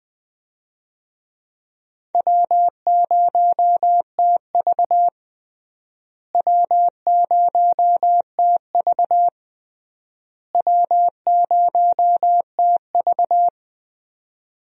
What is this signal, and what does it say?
Morse code, heard as a keyed tone, about 21 wpm, 700 Hz. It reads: W0TV W0TV W0TV
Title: USA Call Signs - 4 Characters - Repeated 3x - 20wpm